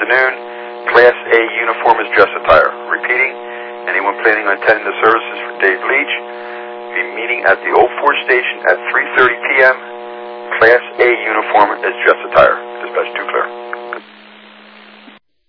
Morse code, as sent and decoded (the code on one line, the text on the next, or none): none